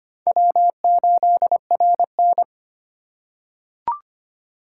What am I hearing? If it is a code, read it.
Morse code, heard as a keyed tone, about 25 wpm, 700 Hz. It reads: W8RN
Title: USA Call Signs - 4 Characters - 6x Character Spacing - 25wpm